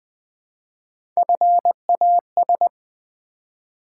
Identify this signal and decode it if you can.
Morse code, heard as a keyed tone, about 20 wpm, 700 Hz. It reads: FAS
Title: Build A Word - 3 Letters At A Time - 6-Letter Words - 20wpm